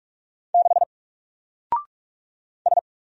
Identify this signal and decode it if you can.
Morse code, heard as a keyed tone, about 46 wpm, 700 Hz. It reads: 6 S
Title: Single Letter-Number Alt 45wpm